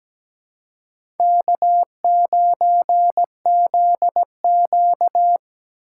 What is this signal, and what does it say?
Morse code, heard as a keyed tone, about 17 wpm, 700 Hz. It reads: K9ZQ
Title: USA Call Signs - 4 Characters - 4x Character Spacing - 17wpm